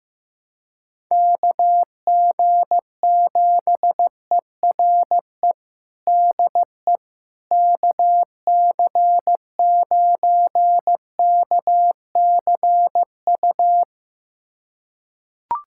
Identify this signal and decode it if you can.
Morse code, heard as a keyed tone, about 15 wpm, 700 Hz. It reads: KG7ERE DE KC9KCU
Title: USA Callsign DE Callsign 15wpm